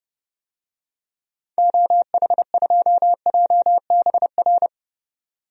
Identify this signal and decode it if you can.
Morse code, heard as a keyed tone, about 31 wpm, 700 Hz. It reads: OH2JBR
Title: DXCC-100 Call Signs - 6 Characters - 30wpm